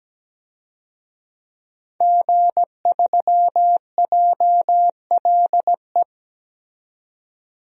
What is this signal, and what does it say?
Morse code, heard as a keyed tone, about 17 wpm, 700 Hz. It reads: G3JLE
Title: DXCC-100 Call Signs - 5 Characters - 17wpm